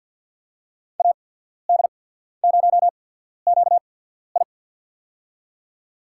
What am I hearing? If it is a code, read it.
Morse code, heard as a keyed tone, about 53 wpm, spaced as for 14 wpm, 700 Hz. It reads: AD0QI
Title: USA Call Signs - 5 Characters - 8x Character Spacing - 50wpm